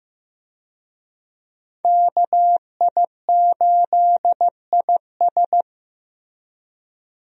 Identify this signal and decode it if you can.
Morse code, heard as a keyed tone, about 15 wpm, 700 Hz. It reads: KI8IS
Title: USA Call Signs - 5 Characters - Repeated 3x - 15wpm